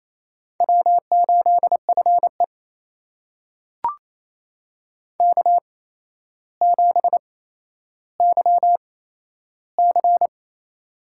Morse code, heard as a keyed tone, about 28 wpm, spaced as for 12 wpm, 700 Hz. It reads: W8FE K7YC
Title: USA Call Signs - 4 Characters - 8x Character Spacing - 28wpm